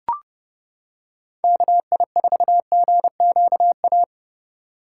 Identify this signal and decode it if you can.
Morse code, heard as a keyed tone, about 30 wpm, 700 Hz. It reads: KI4GQA